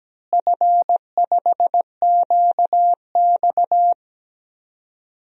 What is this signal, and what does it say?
Morse code, heard as a keyed tone, about 17 wpm, 700 Hz. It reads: F5QX
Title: DXCC-100 Call Signs - 4 Characters - Repeated 2x - 17wpm